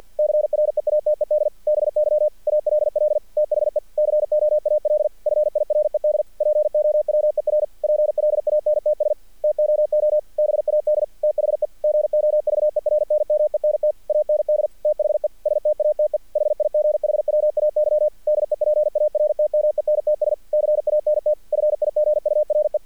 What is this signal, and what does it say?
Morse code, heard as a keyed tone, about 48 wpm, 600 Hz. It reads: CREATED BY ALL THE COAL FIRED POWER PLANTS TOO BAD THE GOVERNMENT AND THE STATE HIGHWAY DEPARTMENTS CANT FIGURE
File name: archived / CRSnet